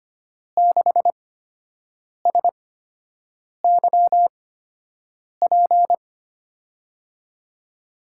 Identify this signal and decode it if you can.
Morse code, heard as a keyed tone, about 25 wpm, spaced as for 7 wpm, 700 Hz. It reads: 6SYP
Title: USA Call Signs - 5 Characters - 8x Character Spacing - 25wpm